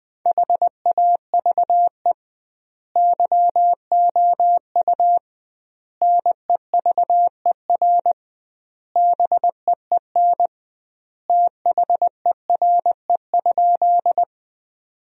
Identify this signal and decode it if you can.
Morse code, heard as a keyed tone, about 21 wpm, 700 Hz. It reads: HAVE YOU NEVER BEEN THERE?
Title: 2x Word-Spacing - Sentences from Top 300 Words - 20wpm